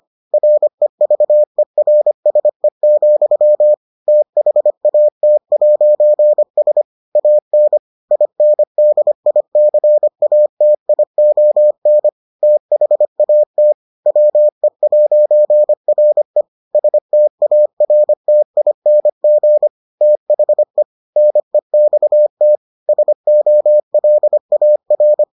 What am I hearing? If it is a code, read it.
Morse code, heard as a keyed tone, about 25 wpm, 600 Hz. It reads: REVERSE, THAT'S AN INDICATION THAT WE'RE STARTING THE NEXT SOLAR